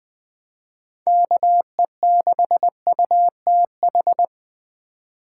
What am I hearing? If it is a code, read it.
Morse code, heard as a keyed tone, about 20 wpm, 700 Hz. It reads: KE6UTH
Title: USA Call Signs - 6 Characters - 2x Character Spacing - 20wpm